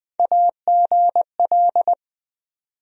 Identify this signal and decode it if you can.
Morse code, heard as a keyed tone, about 20 wpm, 700 Hz. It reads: AGL
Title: USA Call Sign Postfixes - 3 Letters 20wpm